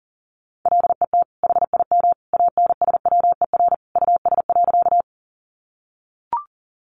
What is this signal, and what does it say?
Morse code, heard as a keyed tone, about 40 wpm, 700 Hz. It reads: LET HIM ANSWER US.
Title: Sentences from Top 300 Words 40wpm